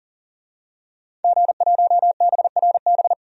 Morse code, heard as a keyed tone, about 40 wpm, 700 Hz. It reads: G1BRB
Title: DXCC-100 Call Signs - 5 Characters - 40wpm